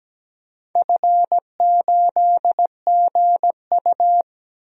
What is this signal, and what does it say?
Morse code, heard as a keyed tone, about 17 wpm, 700 Hz. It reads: F8GU
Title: DXCC-100 Call Signs - 4 Characters - Repeated 2x - 17wpm